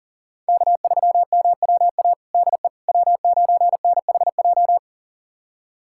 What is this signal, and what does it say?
Morse code, heard as a keyed tone, about 37 wpm, 700 Hz. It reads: K3MWA DE W9NHJ
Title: USA Callsign DE Callsign - 5 Characters - 40wpm